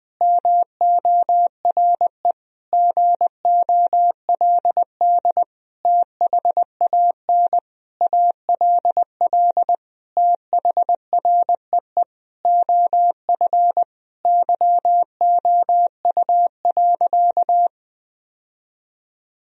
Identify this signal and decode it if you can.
Morse code, heard as a keyed tone, about 20 wpm, 700 Hz. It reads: MORE GOLD THAN ALL THREE OF YOU.
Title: Sentences from Top 1000 Words - Review Part 3 - 20wpm